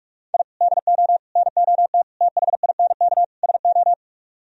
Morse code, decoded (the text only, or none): I DO NOT THINK SO